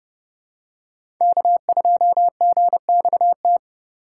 Morse code, heard as a keyed tone, about 30 wpm, 700 Hz. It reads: K2GXT
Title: USA Call Signs - 5 Characters - 2x Character Spacing - 30wpm